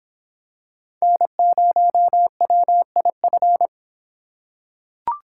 Morse code, heard as a keyed tone, about 25 wpm, 700 Hz. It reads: N0WIF